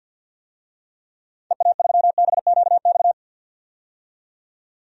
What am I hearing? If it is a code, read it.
Morse code, heard as a keyed tone, about 47 wpm, 700 Hz. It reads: EA3BQX
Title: DXCC-100 Call Signs - 6 Characters - 50wpm